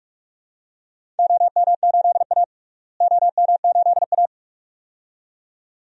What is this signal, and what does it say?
Morse code, heard as a keyed tone, about 41 wpm, 700 Hz. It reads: OM8A OM8A
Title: DXCC-100 Call Signs - 4 Characters - Repeated 2x - 45wpm